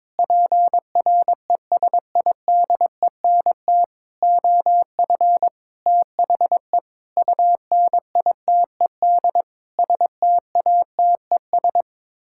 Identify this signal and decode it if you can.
Morse code, heard as a keyed tone, about 22 wpm, 700 Hz. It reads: PRESIDENT OF THE UNITED STATES
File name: Sets of 9 Words - 22wpm